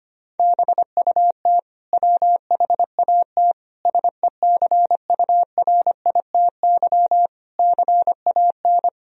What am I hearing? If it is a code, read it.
Morse code, heard as a keyed tone, about 25 wpm, 700 Hz. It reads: BUT WHAT SECURITY CAN
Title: Sentences from Top 1000 Words - Review Part 4 - 25wpm